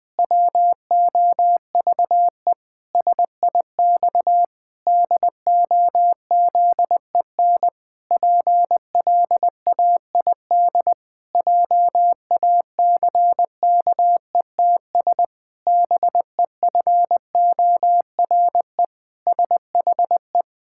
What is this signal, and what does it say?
Morse code, heard as a keyed tone, about 19 wpm, 700 Hz. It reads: WOVE SIX DOZEN PLAID JACKETS BEFORE SHE
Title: Pangram Challenge v2 - 20wpm